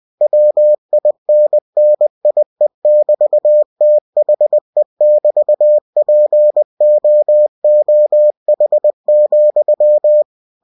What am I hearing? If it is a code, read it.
Morse code, heard as a keyed tone, about 20 wpm, 600 Hz. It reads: WINNIE<BT>THE<BT>POOH,
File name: cw-books / winnie-the-pooh / winnie-20wpm-page-0099